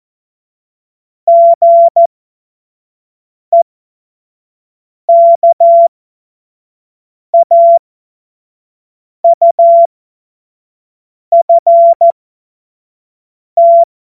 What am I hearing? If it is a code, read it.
Morse code, heard as a keyed tone, about 14 wpm, spaced as for 5 wpm, 680 Hz. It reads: GEKAUFT
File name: Morsekurs / Witze / Witz 12 März 2024 25 F70